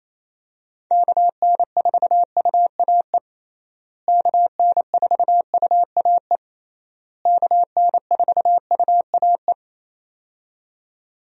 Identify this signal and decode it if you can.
Morse code, heard as a keyed tone, about 28 wpm, 700 Hz. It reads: KN4UAE KN4UAE KN4UAE
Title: USA Call Signs - 6 Characters - Repeated 3x - 28wpm